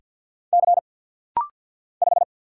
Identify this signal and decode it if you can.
Morse code, heard as a keyed tone, about 50 wpm, 690 Hz. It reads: C 5
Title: Single Letter-Number Alt - Rapid-Fire 50wpm